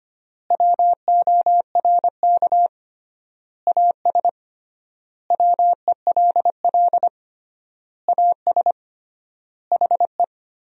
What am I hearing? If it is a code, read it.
Morse code, heard as a keyed tone, about 25 wpm, 700 Hz. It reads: WORK AS WELL AS HE
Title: 3x Word-Spacing - Sentences from Top 500 Words - 25wpm